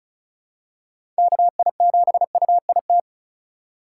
Morse code, heard as a keyed tone, about 36 wpm, 700 Hz. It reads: KI7UIT
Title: USA Call Signs - 6 Characters - 2x Character Spacing - 35wpm